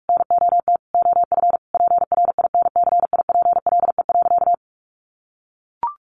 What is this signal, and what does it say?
Morse code, heard as a keyed tone, about 44 wpm, 700 Hz. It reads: NOT OF PRINCIPLE.
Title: Sentences from Top 1400 Words 45wpm